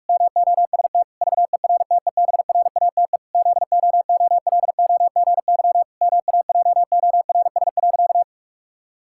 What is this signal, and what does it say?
Morse code, heard as a keyed tone, about 41 wpm, 700 Hz. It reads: MOST VERTEBRATE ZOOLOGY MAJORS.
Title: Pangram Challenge v2 - 45wpm